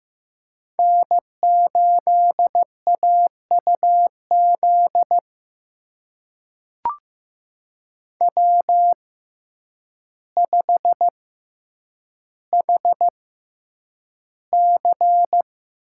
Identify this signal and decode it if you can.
Morse code, heard as a keyed tone, about 15 wpm, spaced as for 8 wpm, 700 Hz. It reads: N8AUZ W5HC
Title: USA Call Signs - 6x Character Spacing - 15wpm